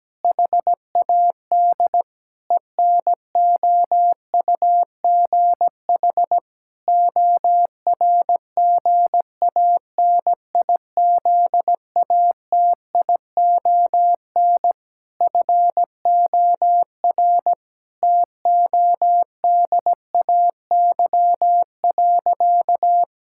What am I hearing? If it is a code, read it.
Morse code, heard as a keyed tone, about 17 wpm, 700 Hz. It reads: HAD ENOUGH ORGANIZATION FOR TODAY.
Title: Sentences from Top 1000 Words - Review Part 5 - 17wpm